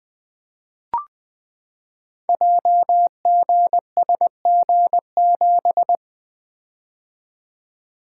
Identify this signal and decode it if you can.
Morse code, heard as a keyed tone, about 20 wpm, 700 Hz. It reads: JGSG7
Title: Code Groups - 5 Letter:Numbers - 20wpm